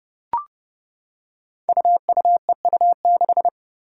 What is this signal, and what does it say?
Morse code, heard as a keyed tone, about 30 wpm, 700 Hz. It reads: UUEU6
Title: Five Challenging Letter-Number Combinations 30wpm